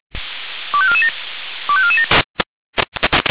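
Five-tone selective call (ZVEI), heard as quick runs of five tones, tones of about 70 ms. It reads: 25447 25447
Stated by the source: ZVEI five-tone selective call received on an SDR.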